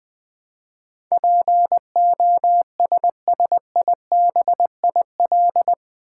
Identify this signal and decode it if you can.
Morse code, heard as a keyed tone, about 20 wpm, 700 Hz. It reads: POSSIBIL
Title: Build A Word - 4 Letters At A Time - 11-Letter Words - 20wpm